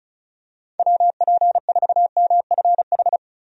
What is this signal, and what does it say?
Morse code, heard as a keyed tone, about 36 wpm, 700 Hz. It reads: WP4MFH
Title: USA Call Signs - 6 Characters - 35wpm